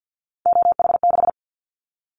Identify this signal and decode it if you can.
Morse code, heard as a keyed tone, about 49 wpm, 700 Hz. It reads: OH6
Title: Callsign Prefixes for Top 25 DXCC - 50wpm